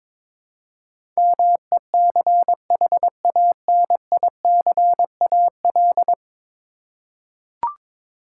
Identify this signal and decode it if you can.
Morse code, heard as a keyed tone, about 21 wpm, 700 Hz. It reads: MECHANICAL
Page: morse-code-ninja-practice-22wpm